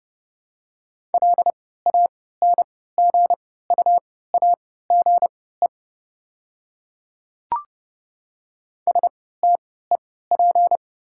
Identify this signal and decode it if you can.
Morse code, heard as a keyed tone, about 30 wpm, spaced as for 17 wpm, 700 Hz. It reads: LANGUAGE STEP
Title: 3x Character Spacing - Top 500 Words - 30wpm